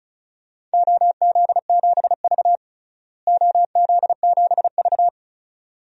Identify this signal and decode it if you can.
Morse code, heard as a keyed tone, about 34 wpm, 700 Hz. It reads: OZ7V OZ7V
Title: DXCC-100 Call Signs - 4 Characters - Repeated 3x - 35wpm